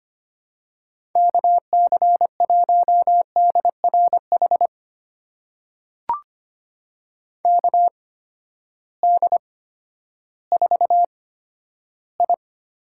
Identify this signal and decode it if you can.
Morse code, heard as a keyed tone, about 25 wpm, spaced as for 12 wpm, 700 Hz. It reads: KC1DRH KD4I
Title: USA Call Signs - 8x Character Spacing - 25wpm